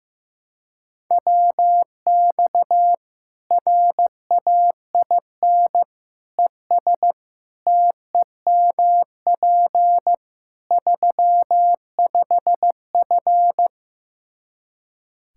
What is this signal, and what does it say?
Morse code, heard as a keyed tone, about 15 wpm, 700 Hz. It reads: WX RAIN ES TEMP 35F
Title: Practice QSO - Exchange 2 - Info - 15wpm